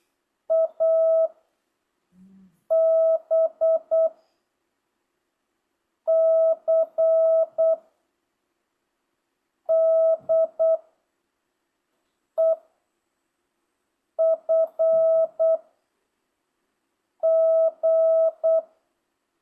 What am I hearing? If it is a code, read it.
Morse code, heard as a keyed tone, about 8 wpm, 660 Hz. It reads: A B C D E F G